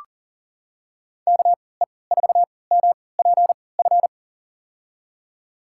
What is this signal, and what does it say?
Morse code, heard as a keyed tone, about 40 wpm, spaced as for 23 wpm, 700 Hz. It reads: KE4MPF
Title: USA Call Signs - 6 Characters - 3x Character Spacing - 40wpm